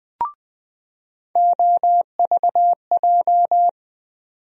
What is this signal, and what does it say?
Morse code, heard as a keyed tone, about 21 wpm, 700 Hz. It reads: OVJ
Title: USA Call Sign Postfixes - 3 Letters 20wpm